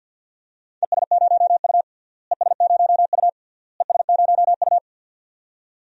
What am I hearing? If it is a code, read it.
Morse code, heard as a keyed tone, about 46 wpm, 700 Hz. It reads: ES0U ES0U ES0U
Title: DXCC-100 Call Signs - 4 Characters - Repeated 3x - 50wpm